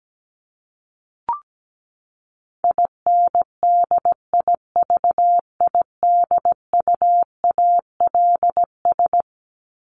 Morse code, heard as a keyed tone, about 17 wpm, 700 Hz. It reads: INDIVIDUALS
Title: ES:S Suffix - Top 100 Words with Base Words 17wpm